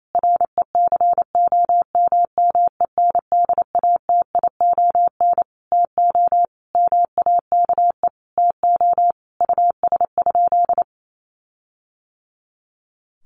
Morse code, heard as a keyed tone, about 28 wpm, 700 Hz. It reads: RECOMMENDATION TO MAKE TO US?
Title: Sentences from Top 1400 Words 28wpm